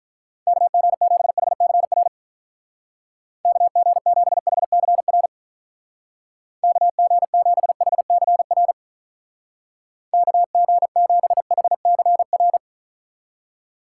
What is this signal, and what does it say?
Morse code, heard as a keyed tone, about 44 wpm, 700 Hz. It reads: KG7HCR KG7HCR KG7HCR KG7HCR